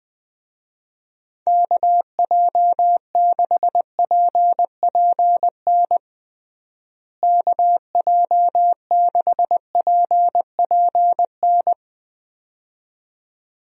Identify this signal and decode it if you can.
Morse code, heard as a keyed tone, about 20 wpm, 700 Hz. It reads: KJ6PPN KJ6PPN
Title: USA Call Signs - 6 Characters - Repeated 2x - 20wpm